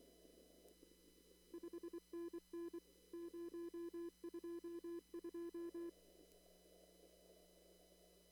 Morse code, heard as a keyed tone, about 24 wpm, 340 Hz. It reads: 5NN 022
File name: CW / kt5j wpx cw-op-k5pi